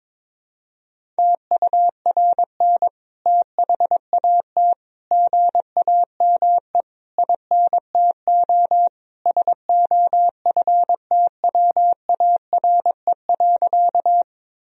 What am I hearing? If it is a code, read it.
Morse code, heard as a keyed tone, about 22 wpm, 700 Hz. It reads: TURN THAT GAME INTO SOFTWARE.
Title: Sentences from Top 1000 Words - Review Part 3 - 22wpm